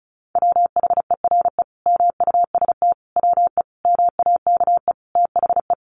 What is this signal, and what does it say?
Morse code, heard as a keyed tone, about 34 wpm, 700 Hz. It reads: WHERE MUST WE MAKE THE
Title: Sentences from Top 1300 Words 35wpm